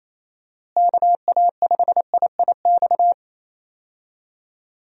Morse code, heard as a keyed tone, about 28 wpm, 700 Hz. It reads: KA5IIX
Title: USA Call Signs - 6 Characters - Repeated 3x - 28wpm